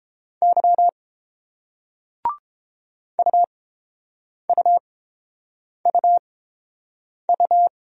Morse code, heard as a keyed tone, about 30 wpm, spaced as for 7 wpm, 700 Hz. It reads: Y UUUU